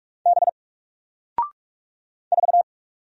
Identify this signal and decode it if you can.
Morse code, heard as a keyed tone, about 44 wpm, 700 Hz. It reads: B 4
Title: Single Character (Letter:Number:Punctuation:Prosign) Alt 45wpm